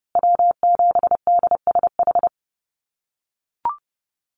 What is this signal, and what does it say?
Morse code, heard as a keyed tone, about 30 wpm, 700 Hz. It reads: W7DSH
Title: USA Call Signs Repeated 3x 30wpm